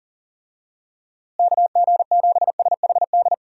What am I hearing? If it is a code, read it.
Morse code, heard as a keyed tone, about 40 wpm, 700 Hz. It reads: KG7SHD
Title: USA Call Signs - 6 Characters - Repeated 2x - 40wpm